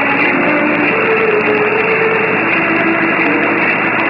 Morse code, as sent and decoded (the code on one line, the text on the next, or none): none